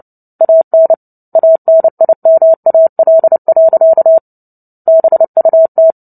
Morse code, heard as a keyed tone, about 30 wpm, 650 Hz. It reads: AN ANIMAL. BUT
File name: Book Two / 03-Chapter 3 - 29 WPM